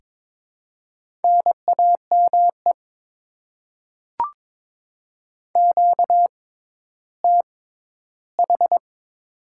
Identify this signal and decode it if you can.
Morse code, heard as a keyed tone, about 22 wpm, spaced as for 12 wpm, 690 Hz. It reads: NAME QTH
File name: Top 20 QSO Elements - 6x Character Spacing - 22wpm